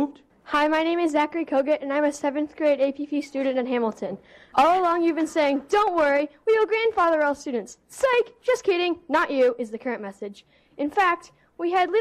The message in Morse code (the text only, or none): none